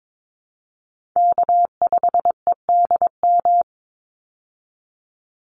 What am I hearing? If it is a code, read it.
Morse code, heard as a keyed tone, about 22 wpm, 700 Hz. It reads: K5EDM
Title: Call Signs Repeated 3x - 22wpm